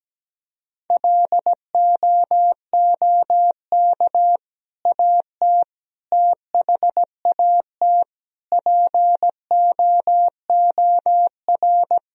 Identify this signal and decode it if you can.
Morse code, heard as a keyed tone, about 17 wpm, 700 Hz. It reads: LOOK AT THAT POOR
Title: Sentences from Top 1000 Words - Review Part 2 - 17wpm